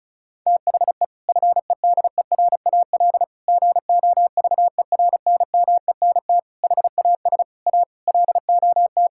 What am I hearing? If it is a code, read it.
Morse code, heard as a keyed tone, about 34 wpm, 700 Hz. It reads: THE FEDERAL GOVERNMENT HAS A LOT
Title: Sentences from Top 1000 Words - Review Part 5 - 35wpm